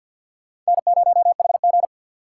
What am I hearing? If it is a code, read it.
Morse code, heard as a keyed tone, about 47 wpm, 700 Hz. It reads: N0HG
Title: USA Call Signs - 4 Characters - 6x Character Spacing - 50wpm